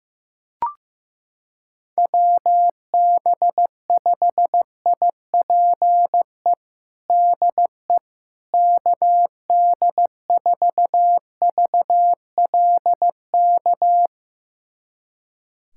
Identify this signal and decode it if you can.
Morse code, heard as a keyed tone, about 15 wpm, 700 Hz. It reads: WB5IPE DE KD4VLK
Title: USA Callsign DE Callsign 15wpm